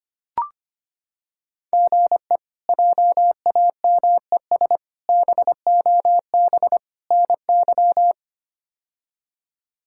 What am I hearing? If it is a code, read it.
Morse code, heard as a keyed tone, about 25 wpm, 700 Hz. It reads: GE JAMES BOB NY
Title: K1USN SST QSO Practice - 25wpm